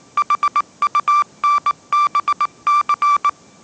Morse code, heard as a keyed tone, about 19 wpm, 1170 Hz. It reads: HUNBC
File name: 20wpm